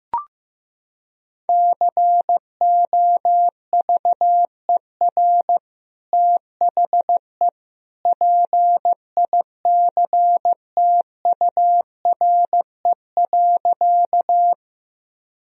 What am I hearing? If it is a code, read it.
Morse code, heard as a keyed tone, about 15 wpm, 700 Hz. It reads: COVER THE PICTURE.
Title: Sentences from Top 1000 Words - Review Part 1 - 15wpm